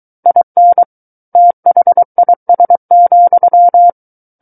Morse code, heard as a keyed tone, about 23 wpm, 700 Hz. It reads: IN THIS,
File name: Book One / 14-Chapter 14 - 23 WPM